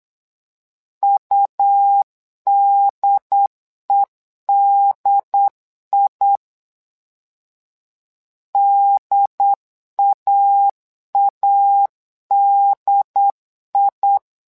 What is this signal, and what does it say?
Morse code, heard as a keyed tone, about 8 wpm, 800 Hz. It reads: UDEDI DAADI
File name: LEZIONI / 2 UHDN / 2 UHD 8wpm 2258